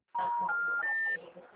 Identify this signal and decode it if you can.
Special information tone, with speech.